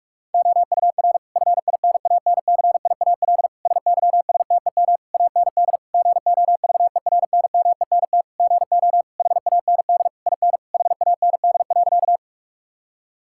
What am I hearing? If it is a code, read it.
Morse code, heard as a keyed tone, about 46 wpm, 700 Hz. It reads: OUR FINANCIAL SYSTEM AND GOVERNMENT GO HAND IN HAND.
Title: Sentences from Top 1000 Words - Review Part 5 - 45wpm